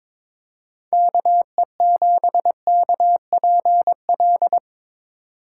Morse code, heard as a keyed tone, about 24 wpm, 700 Hz. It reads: KE7KPL
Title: Speed Racing - USA Call Signs 22wpm